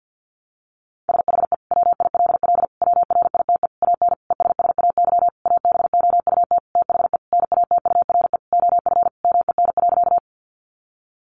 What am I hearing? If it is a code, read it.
Morse code, heard as a keyed tone, about 50 wpm, 700 Hz. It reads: SHE WILL WRITE AN ESSAY ABOUT THE NATURE OF MEN.
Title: Sentences from Top 1500 Words 50wpm